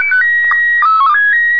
SSTV (slow-scan television), PD 180.